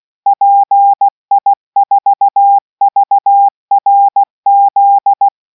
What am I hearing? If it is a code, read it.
Morse code, heard as a keyed tone, about 16 wpm, 800 Hz. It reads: PI4VRZ